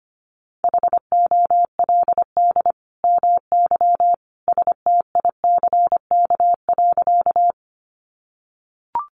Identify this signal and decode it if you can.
Morse code, heard as a keyed tone, about 25 wpm, 700 Hz. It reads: HOLD MY STICK.
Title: Sentences from Top 1200 Words 25wpm